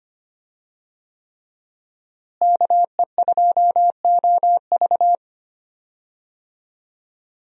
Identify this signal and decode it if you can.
Morse code, heard as a keyed tone, about 25 wpm, 700 Hz. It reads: KE2OV
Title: USA Call Signs - 5 Characters - Repeated 3x - 25wpm